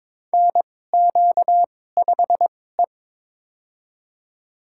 Morse code, heard as a keyed tone, about 22 wpm, spaced as for 16 wpm, 700 Hz. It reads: NQ5E
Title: USA Call Signs - 2x Character Spacing - 22wpm